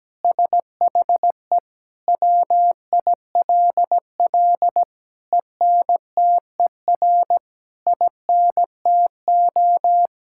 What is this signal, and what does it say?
Morse code, heard as a keyed tone, about 17 wpm, 700 Hz. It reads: SHE WILL ENTER INTO
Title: Sentences from Top 1000 Words - Review Part 6 - 17wpm